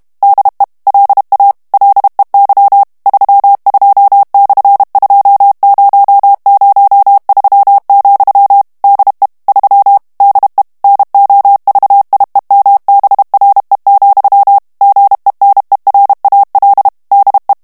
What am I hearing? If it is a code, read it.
Morse code, heard as a keyed tone, about 32 wpm, 800 Hz. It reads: DE LA LEY 32/2003, DE 3 DE NOVIEMBRE, GENERAL DE
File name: ESPAGNOL TEXTE EN CLAIR 2 VITESSE 30 MOTS MINUTES